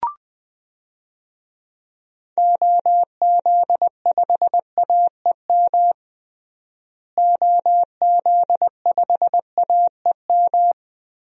Morse code, heard as a keyed tone, about 19 wpm, 700 Hz. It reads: OZ5AEM OZ5AEM
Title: DXCC-100 Call Signs - 6 Characters - Repeated 3x - 20wpm